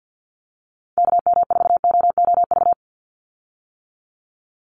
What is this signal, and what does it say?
Morse code, heard as a keyed tone, about 50 wpm, 700 Hz. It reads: KM4OOV
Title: USA Call Signs 50wpm